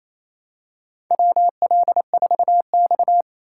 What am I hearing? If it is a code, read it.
Morse code, heard as a keyed tone, about 28 wpm, 700 Hz. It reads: WL4X
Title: USA Call Signs - 4 Characters - 2x Character Spacing - 28wpm